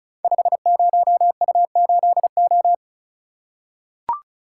Morse code, heard as a keyed tone, about 36 wpm, 700 Hz. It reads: 50U8O
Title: Code Groups - 5 Letter:Numbers - 35wpm